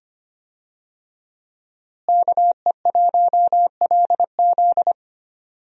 Morse code, heard as a keyed tone, about 25 wpm, 700 Hz. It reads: KE1LZ
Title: USA Call Signs - 5 Characters - Repeated 2x - 25wpm